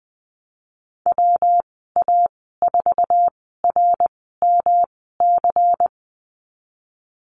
Morse code, heard as a keyed tone, about 20 wpm, 700 Hz. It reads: W A 4 R M C
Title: 231 Lesson 37g - Letter x - Call Signs 20wpm